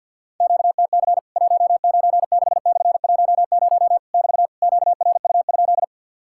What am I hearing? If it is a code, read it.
Morse code, heard as a keyed tone, about 50 wpm, 700 Hz. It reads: QTC 196/10 <BT> QRU?
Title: Worked All Europe DX Contest - QTC Exchange - 50wpm